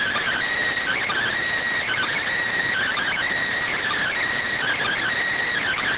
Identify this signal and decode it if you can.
Emergency Alert System (SAME) tones (header bursts), heard as harsh buzzing bursts. It is heard from 0.8 to 5.1 s.
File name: findings / CT AXIAL -572mm 2.gen-38 rx